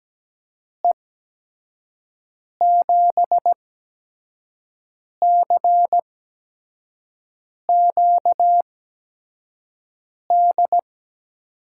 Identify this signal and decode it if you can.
Morse code, heard as a keyed tone, about 17 wpm, spaced as for 5 wpm, 700 Hz. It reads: E7CQD
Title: USA Call Signs - 6 Characters - 8x Character Spacing - 17wpm